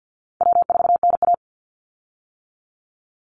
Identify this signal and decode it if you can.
Morse code, heard as a keyed tone, about 50 wpm, 700 Hz. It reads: W4NA
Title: Easy Call Signs - 50wpm